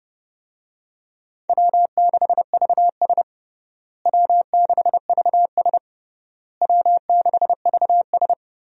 Morse code, heard as a keyed tone, about 32 wpm, 700 Hz. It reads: W6VS W6VS W6VS
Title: USA Call Signs - 4 Characters - Repeated 3x - 30wpm